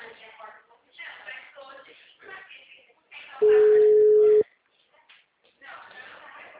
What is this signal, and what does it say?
Ringback tone (Europe, 425 Hz), with speech.
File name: out- -2017- - - .78547